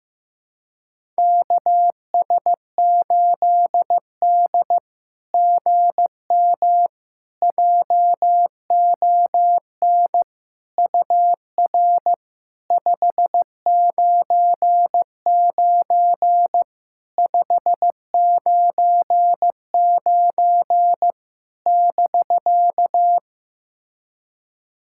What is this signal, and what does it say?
Morse code, heard as a keyed tone, about 15 wpm, 700 Hz. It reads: KS8D GM JON UR 599 599 <BK>
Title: SOTA QSO Practice 15wpm